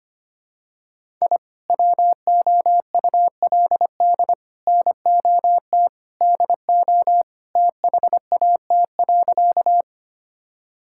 Morse code, heard as a keyed tone, about 25 wpm, 700 Hz. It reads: I WOULD NOT DO THAT.
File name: practice / 25wpm / Sentences with 3-Word Phrases - 25wpm